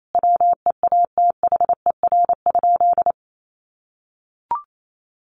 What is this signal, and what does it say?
Morse code, heard as a keyed tone, about 28 wpm, 700 Hz. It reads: WEATHER?
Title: Sentences from Top 1100 Words 28wpm